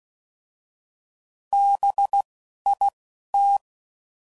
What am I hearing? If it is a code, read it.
Morse code, heard as a keyed tone, about 16 wpm, 800 Hz. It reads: B I T